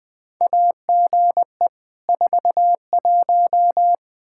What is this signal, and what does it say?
Morse code, heard as a keyed tone, about 20 wpm, 700 Hz. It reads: AGE 41
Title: K1USN SST QSO Practice 20wpm